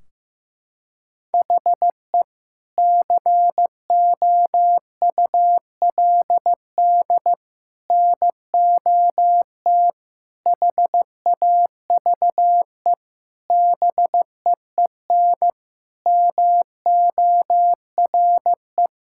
Morse code, heard as a keyed tone, about 15 wpm, 700 Hz. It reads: HE COULD NOT HAVE BEEN MORE
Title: Sentences from Top 1600 Words 15wpm